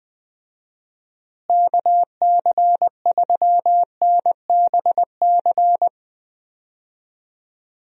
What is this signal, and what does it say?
Morse code, heard as a keyed tone, about 21 wpm, 700 Hz. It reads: KC3NBC
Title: USA Call Signs - 6 Characters - Repeated 2x - 20wpm